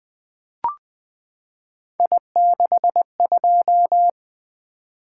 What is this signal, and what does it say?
Morse code, heard as a keyed tone, about 20 wpm, 700 Hz. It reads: I62
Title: Three Letter-Number Combinations - 20wpm